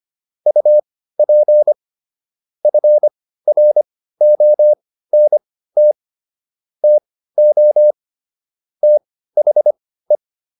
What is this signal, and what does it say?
Morse code, heard as a keyed tone, about 25 wpm, spaced as for 15 wpm, 600 Hz. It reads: UP FRONT TO THE